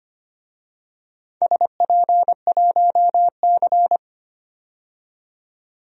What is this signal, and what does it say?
Morse code, heard as a keyed tone, about 25 wpm, 700 Hz. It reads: SP1C
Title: DXCC-100 Call Signs - 4 Characters - 25wpm